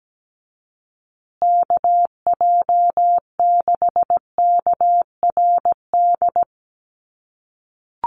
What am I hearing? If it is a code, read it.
Morse code, heard as a keyed tone, about 17 wpm, 700 Hz. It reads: KJ6KRD
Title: USA Call Signs Repeated 2x 17wpm